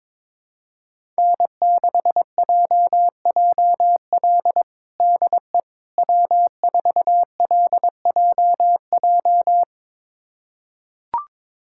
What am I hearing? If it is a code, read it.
Morse code, heard as a keyed tone, about 22 wpm, 700 Hz. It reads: N6JJL DE W4LJJ
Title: USA Callsign DE Callsign - 5 Characters - 22wpm